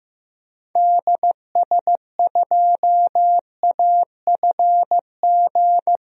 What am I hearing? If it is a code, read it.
Morse code, heard as a keyed tone, about 16 wpm, 700 Hz. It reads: DS2AFG
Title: DXCC-100 Call Signs - 6 Characters - Repeated 2x - 15wpm